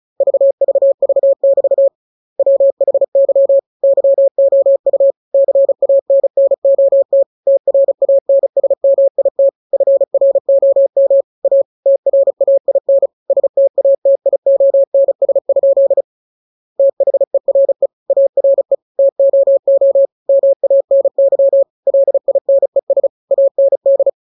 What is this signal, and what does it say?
Morse code, heard as a keyed tone, about 34 wpm, 550 Hz. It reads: VVV<BT> WHY YOU CANNOT TRANSMIT FROM A TRAIN STATIONS? THERE ARE TOO MANY LINES AND
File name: jokes / joke 20 35 0000